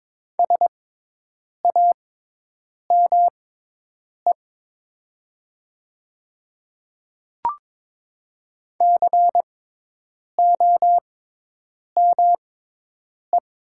Morse code, heard as a keyed tone, about 22 wpm, spaced as for 8 wpm, 700 Hz. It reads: SAME COME
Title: 6x Character Spacing - Top 200 Words - 22wpm